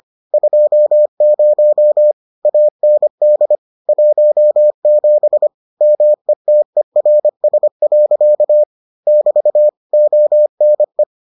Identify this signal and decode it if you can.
Morse code, heard as a keyed tone, about 25 wpm, 600 Hz. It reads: 20 AND 17 METERS. <BT> ONE